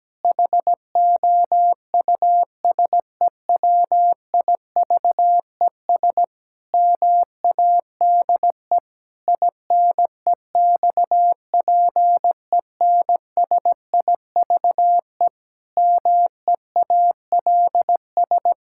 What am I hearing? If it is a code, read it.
Morse code, heard as a keyed tone, about 17 wpm, 700 Hz. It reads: HOUSEWIVES MADE INEXPENSIVE MEALS
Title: Pangram Challenge v2 - 17wpm